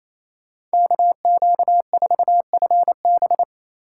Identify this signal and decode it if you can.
Morse code, heard as a keyed tone, about 28 wpm, 700 Hz. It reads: KQ4FB
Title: USA Call Signs - 5 Characters - 4x Character Spacing - 28wpm